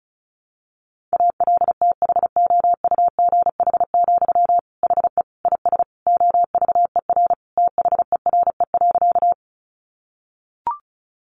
Morse code, heard as a keyed tone, about 34 wpm, 700 Hz. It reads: ALTHOUGH, HE IS OVER THERE.
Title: Sentences from Top 600 Words 35wpm